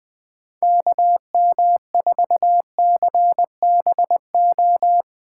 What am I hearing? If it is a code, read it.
Morse code, heard as a keyed tone, about 20 wpm, 700 Hz. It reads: KM4CBO